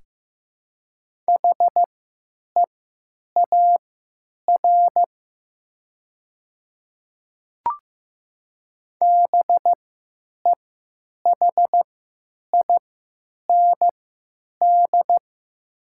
Morse code, heard as a keyed tone, about 15 wpm, spaced as for 9 wpm, 700 Hz. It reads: HEAR BEHIND
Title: 3x Character Spacing - Top 500 Words - 15wpm